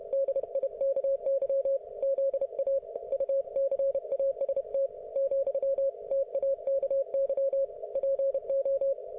Morse code, heard as a keyed tone, about 31 wpm, 530 Hz. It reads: DIKY ZA UCAST ,TAKY PO